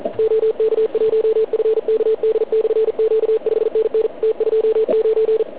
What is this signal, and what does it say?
Morse code, heard as a keyed tone, about 44 wpm, 440 Hz. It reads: OK1FKD/Q 5NN T29